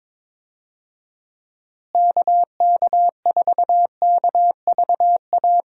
Morse code, heard as a keyed tone, about 22 wpm, 700 Hz. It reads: KK4KVA
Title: USA Call Signs - 6 Characters - Repeated 2x - 22wpm